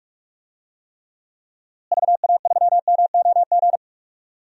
Morse code, heard as a keyed tone, about 45 wpm, 700 Hz. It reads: VA3MOG
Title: DXCC-100 Call Signs - 6 Characters - Repeated 2x - 45wpm